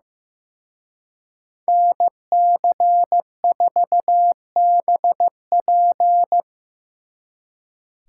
Morse code, heard as a keyed tone, about 18 wpm, 700 Hz. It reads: NC4BP
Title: Speed Racing - USA 2x2 Call Signs 15wpm